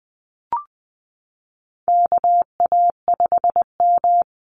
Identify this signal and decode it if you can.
Morse code, heard as a keyed tone, about 20 wpm, 700 Hz. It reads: KA5M
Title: Call Signs Repeated 2x - 20wpm